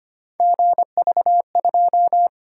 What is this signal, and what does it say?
Morse code, heard as a keyed tone, about 25 wpm, 700 Hz. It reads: GV2
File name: Three Letter-Number Combinations - 25wpm